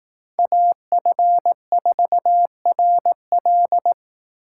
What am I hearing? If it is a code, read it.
Morse code, heard as a keyed tone, about 18 wpm, 700 Hz. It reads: AF4RL
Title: Speed Racing - USA 2x2 Call Signs 15wpm